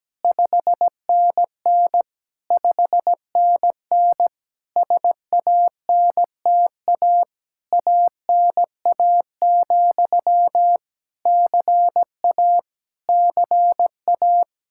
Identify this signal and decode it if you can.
Morse code, heard as a keyed tone, about 17 wpm, 700 Hz. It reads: 5NN 5NN SANTA ANA, CA CA
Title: Practice QSO - Exchange 1 - Beginning - 17wpm